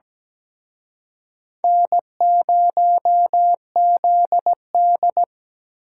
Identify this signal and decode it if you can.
Morse code, heard as a keyed tone, about 20 wpm, 700 Hz. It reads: N0ZD
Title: Speed Racing - USA 1x2 Call Signs 17wpm